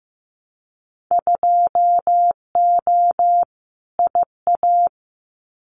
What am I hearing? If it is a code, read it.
Morse code, heard as a keyed tone, about 15 wpm, 700 Hz. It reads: 2O IA
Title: Winter Field Day Exchanges 15wpm